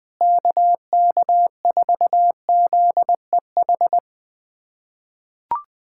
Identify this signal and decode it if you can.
Morse code, heard as a keyed tone, about 20 wpm, 700 Hz. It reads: KK4ZEH
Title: SOTA QSO Practice 20wpm